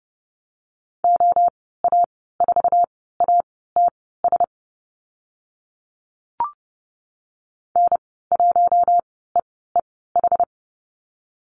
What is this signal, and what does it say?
Morse code, heard as a keyed tone, about 30 wpm, spaced as for 17 wpm, 700 Hz. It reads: OA4ATS N1EEH
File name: Lesson 14g - Number 2 - Call Signs 30wpm